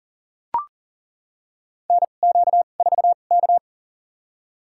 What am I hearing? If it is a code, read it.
Morse code, heard as a keyed tone, about 38 wpm, spaced as for 28 wpm, 690 Hz. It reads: NQ4K
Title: USA Call Signs - 4 Characters - 2x Character Spacing - 40wpm